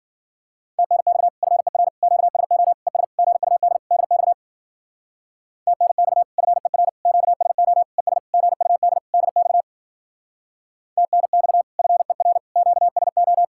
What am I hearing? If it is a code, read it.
Morse code, heard as a keyed tone, about 53 wpm, 700 Hz. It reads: TNX FER QSO ES GUD DX TNX FER QSO ES GUD DX TNX FER QSO